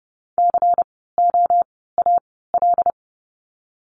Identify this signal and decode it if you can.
Morse code, heard as a keyed tone, about 31 wpm, spaced as for 17 wpm, 700 Hz. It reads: COAL
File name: Lesson 16f - Letter c - Words 30wpm